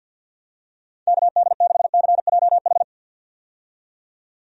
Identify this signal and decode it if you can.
Morse code, heard as a keyed tone, about 48 wpm, 700 Hz. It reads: KD6CJH
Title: USA Call Signs - 6 Characters - 50wpm